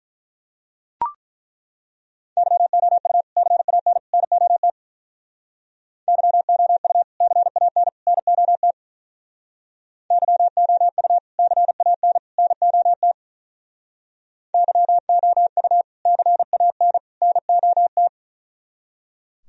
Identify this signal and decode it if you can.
Morse code, heard as a keyed tone, about 42 wpm, 700 Hz. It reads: YOU CAN NOT YOU CAN NOT YOU CAN NOT YOU CAN NOT
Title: Speed-Racing - Sets of 3 Words 35wpm